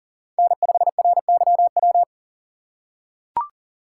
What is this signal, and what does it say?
Morse code, heard as a keyed tone, about 40 wpm, 700 Hz. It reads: N5RYW
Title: USA Call Signs - 5 Characters - 3x Character Spacing - 40wpm